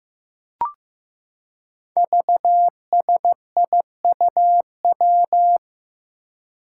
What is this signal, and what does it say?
Morse code, heard as a keyed tone, about 15 wpm, 700 Hz. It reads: VSIUW
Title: Five Challenging Letter-Number Combinations 15wpm